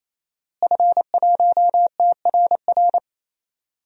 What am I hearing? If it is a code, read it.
Morse code, heard as a keyed tone, about 28 wpm, 700 Hz. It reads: F1TRR
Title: DXCC-100 Call Signs - 5 Characters - Repeated 3x - 28wpm